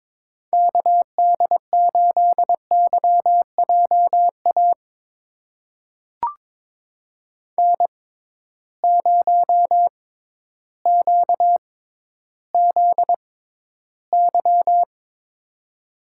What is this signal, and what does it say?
Morse code, heard as a keyed tone, about 22 wpm, spaced as for 12 wpm, 700 Hz. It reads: KD8YJA N0QZY
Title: USA Call Signs - 6x Character Spacing - 22wpm